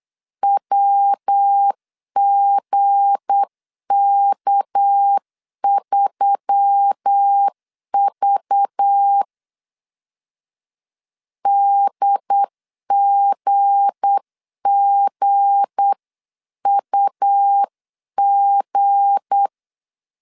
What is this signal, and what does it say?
Morse code, heard as a keyed tone, about 8 wpm, 800 Hz. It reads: WGK3V DGGUG
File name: LEZIONI / 7 GF9Z / 7 G 8wpm 3118